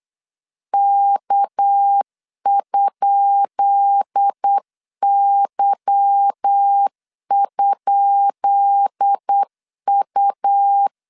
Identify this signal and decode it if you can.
Morse code, heard as a keyed tone, about 8 wpm, 800 Hz. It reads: K?Y?U
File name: LEZIONI / 10 SIGNS / 10B INT 8wpm 3534